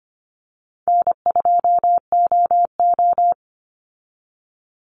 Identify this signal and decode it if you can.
Morse code, heard as a keyed tone, about 25 wpm, 700 Hz. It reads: N2OO
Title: Easy Call Signs - 25wpm